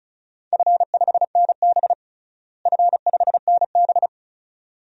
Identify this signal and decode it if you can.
Morse code, heard as a keyed tone, about 36 wpm, 700 Hz. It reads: F5NB F5NB
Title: DXCC-100 Call Signs - 4 Characters - Repeated 2x - 35wpm